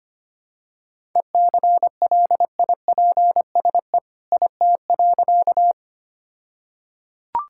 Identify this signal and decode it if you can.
Morse code, heard as a keyed tone, about 25 wpm, 700 Hz. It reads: ECLIPSE IT.
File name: practice / 25wpm / All-Age Appropriate Jokes - 25wpm